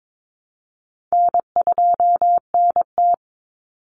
Morse code, heard as a keyed tone, about 22 wpm, 700 Hz. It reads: N2NT
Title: Easy Call Signs Repeated 2x - 22wpm